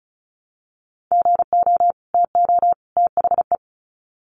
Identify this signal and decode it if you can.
Morse code, heard as a keyed tone, about 34 wpm, 700 Hz. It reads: GO TO THE
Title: Sets of 3 Words 35wpm